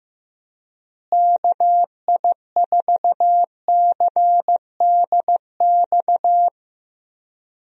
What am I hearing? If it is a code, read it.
Morse code, heard as a keyed tone, about 15 wpm, 700 Hz. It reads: KI4CDX
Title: USA Call Signs - 6 Characters - Repeated 2x - 15wpm